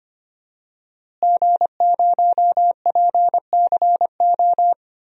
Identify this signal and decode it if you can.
Morse code, heard as a keyed tone, about 25 wpm, 700 Hz. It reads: G0PCO
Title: DXCC-100 Call Signs - 5 Characters - Repeated 3x - 25wpm